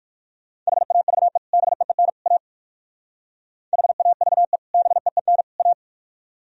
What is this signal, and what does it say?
Morse code, heard as a keyed tone, about 63 wpm, spaced as for 56 wpm, 700 Hz. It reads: HAVE BEEN A HAVE BEEN A
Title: Speed-Racing - Sets of 3 Words 45wpm